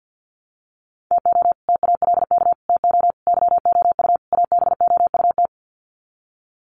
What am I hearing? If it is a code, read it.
Morse code, heard as a keyed tone, about 50 wpm, 700 Hz. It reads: TO TALK TO YOU ABOUT
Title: Sets of 5 Words Encore 50wpm